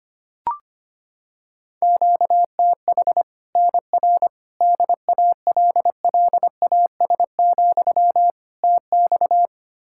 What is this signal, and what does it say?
Morse code, heard as a keyed tone, about 25 wpm, 700 Hz. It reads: QTH NR DALLAS, TX
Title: Practice QSO - Exchange 1 - Beginning - 25wpm